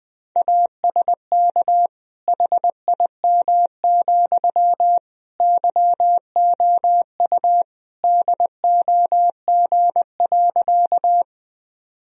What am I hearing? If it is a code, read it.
Morse code, heard as a keyed tone, about 20 wpm, 700 Hz. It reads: ASK HIM, YOU DOG.
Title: Sentences from Top 1000 Words - Review Part 2 - 20wpm